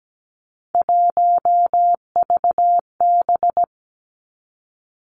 Morse code, heard as a keyed tone, about 17 wpm, 700 Hz. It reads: 1VB
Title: Three Challenging Letter-Number Combinations 17wpm